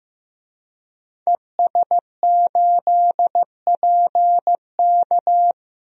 Morse code, heard as a keyed tone, about 15 wpm, 700 Hz. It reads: ES8PK
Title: DXCC-100 Call Signs - 5 Characters - 15wpm